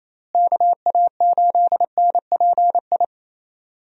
Morse code, heard as a keyed tone, about 27 wpm, 700 Hz. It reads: KA8NPI
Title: USA Call Signs - 6 Characters - Repeated 2x - 28wpm